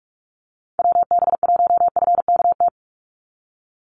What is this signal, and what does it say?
Morse code, heard as a keyed tone, about 44 wpm, 700 Hz. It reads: WB1FKT
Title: USA Call Signs 45wpm